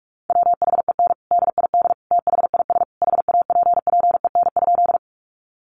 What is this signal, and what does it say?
Morse code, heard as a keyed tone, about 44 wpm, 700 Hz. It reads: WHEN DID THIS HAPPEN?
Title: Sentences from Top 400 Words 45wpm